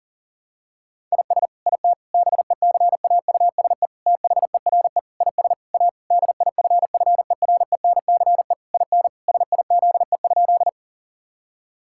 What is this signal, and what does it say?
Morse code, heard as a keyed tone, about 40 wpm, 700 Hz. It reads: IS IT BECAUSE THERE IS A DIFFERENCE IN SIZE?
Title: Sentences from Top 1000 Words - Review Part 5 - 40wpm